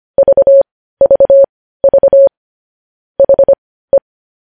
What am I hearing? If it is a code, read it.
Morse code, heard as a keyed tone, about 25 wpm, spaced as for 15 wpm, 550 Hz. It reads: VVV HE
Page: news headlines in morse code at 15 wpm